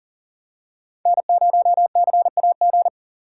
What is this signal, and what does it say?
Morse code, heard as a keyed tone, about 40 wpm, 700 Hz. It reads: N0CAG
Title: USA Call Signs - 5 Characters - 4x Character Spacing - 40wpm